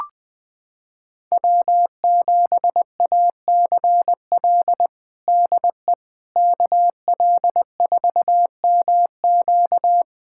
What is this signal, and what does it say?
Morse code, heard as a keyed tone, about 20 wpm, 700 Hz. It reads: W7ACL DE KL4MQ
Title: USA Callsign DE Callsign - 5 Characters - 20wpm